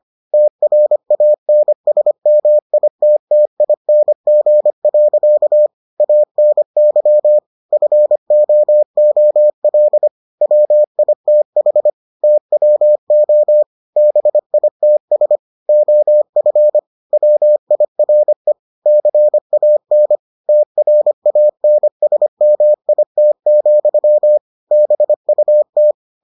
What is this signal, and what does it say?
Morse code, heard as a keyed tone, about 25 wpm, 600 Hz. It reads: TRANSMITTING. ANY FOOL WITH TWO BITS OF WIRE CAN TRANSMIT, BUT